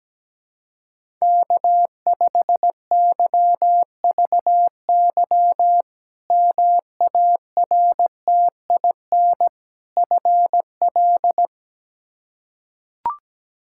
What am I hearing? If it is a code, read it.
Morse code, heard as a keyed tone, about 17 wpm, 700 Hz. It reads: K5YVY MARTIN FL
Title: K1USN SST QSO Practice - 17wpm